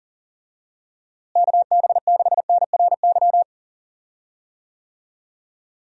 Morse code, heard as a keyed tone, about 40 wpm, 700 Hz. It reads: KB6NRY
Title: USA Call Signs - 6 Characters - Repeated 3x - 40wpm